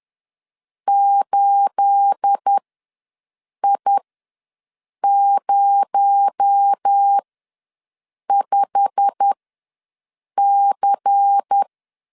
Morse code, heard as a keyed tone, about 10 wpm, spaced as for 6 wpm, 800 Hz. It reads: 8I05C
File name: LEZIONI / 9 28 ws 10wpm 4124